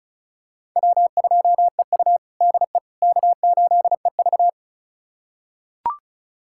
Morse code, heard as a keyed tone, about 34 wpm, 700 Hz. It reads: W2EU DE K8EV
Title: USA Callsign DE Callsign - 4 Characters - 35wpm